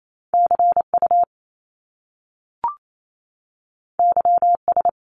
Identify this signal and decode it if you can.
Morse code, heard as a keyed tone, about 28 wpm, 700 Hz. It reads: CU YS
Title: Top Two Letter Combinations in Words 28wpm